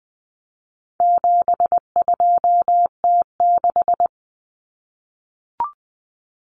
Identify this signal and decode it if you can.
Morse code, heard as a keyed tone, about 20 wpm, 700 Hz. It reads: 72T6
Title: Common Cut Numbers - 4-Digit 20wpm